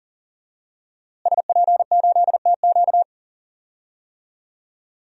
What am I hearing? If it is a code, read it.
Morse code, heard as a keyed tone, about 40 wpm, 700 Hz. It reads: SP8TQ
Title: DXCC-100 Call Signs - 5 Characters - Repeated 2x - 40wpm